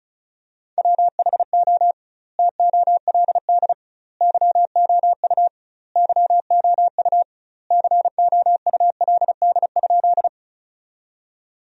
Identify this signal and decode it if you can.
Morse code, heard as a keyed tone, about 34 wpm, 700 Hz. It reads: WHO TOLD YOU YOU COULD?
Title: 2x Word-Spacing - Sentences from Top 400 Words - 35wpm